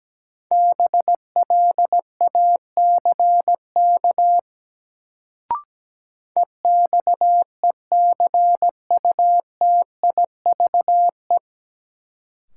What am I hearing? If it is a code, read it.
Morse code, heard as a keyed tone, about 17 wpm, 700 Hz. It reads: BLACK EXECUTIVE
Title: Sets of 5 Words Encore - New Word Drill - 17wpm